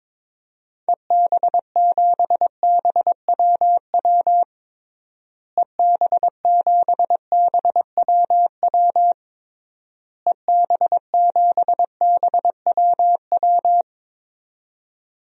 Morse code, heard as a keyed tone, about 22 wpm, 700 Hz. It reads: EB7BWW EB7BWW EB7BWW
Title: DXCC-100 Call Signs - 6 Characters - Repeated 3x - 22wpm